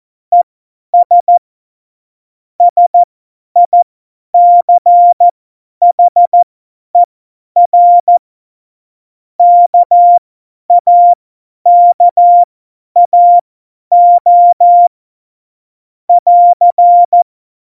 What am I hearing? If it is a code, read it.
Morse code, heard as a keyed tone, about 14 wpm, spaced as for 10 wpm, 700 Hz. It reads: ES SICHER KAKAO <AR>
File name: Morsekurs / Witze / Witz 18. Juni 2019, 50 F70